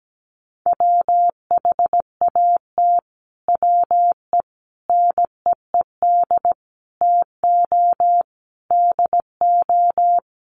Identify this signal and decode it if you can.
Morse code, heard as a keyed tone, about 17 wpm, 700 Hz. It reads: WHAT WE NEED TO DO
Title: Sets of 5 Words Encore 17wpm